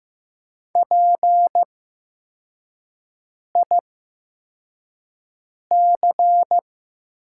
Morse code, heard as a keyed tone, about 15 wpm, spaced as for 4 wpm, 700 Hz. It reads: PIC
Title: 8x Character Spacing - Top 300 Words - 15wpm